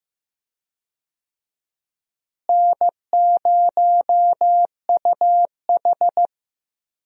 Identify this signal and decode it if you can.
Morse code, heard as a keyed tone, about 15 wpm, 700 Hz. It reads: N0UH
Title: USA Call Signs - 4 Characters - Repeated 2x - 15wpm